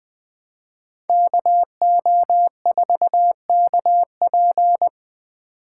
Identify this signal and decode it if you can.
Morse code, heard as a keyed tone, about 20 wpm, 700 Hz. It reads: KO4KP
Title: Speed Racing - USA 2x2 Call Signs 17wpm